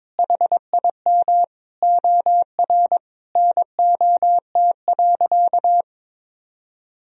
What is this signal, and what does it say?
Morse code, heard as a keyed tone, about 22 wpm, 700 Hz. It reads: HIM OR NOT.
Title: Sentences from Top 1000 Words - Review Part 3 - 22wpm